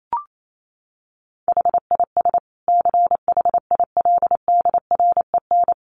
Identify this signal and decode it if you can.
Morse code, heard as a keyed tone, about 28 wpm, 700 Hz. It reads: HIS CHILDREN
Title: Sentences from Top 900 Words 28wpm